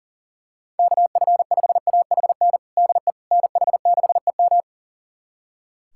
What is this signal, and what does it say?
Morse code, heard as a keyed tone, about 40 wpm, 700 Hz. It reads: KF5AHN DE NH6EM
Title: USA Callsign DE Callsign 40wpm